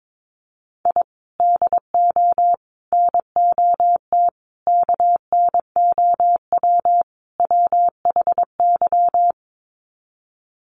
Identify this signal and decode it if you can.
Morse code, heard as a keyed tone, about 22 wpm, 700 Hz. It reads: I DO NOT KNOW WHY
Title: Sets of 5 Words 22wpm